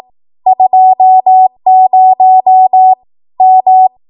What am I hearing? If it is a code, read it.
Morse code, heard as a keyed tone, about 18 wpm, 750 Hz. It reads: 20 M
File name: Morse / 18WPM